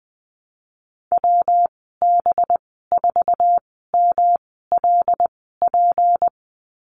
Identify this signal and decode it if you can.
Morse code, heard as a keyed tone, about 20 wpm, spaced as for 14 wpm, 700 Hz. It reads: WB4MLP
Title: 193 Lesson 31g - Letter v - Call Signs 20wpm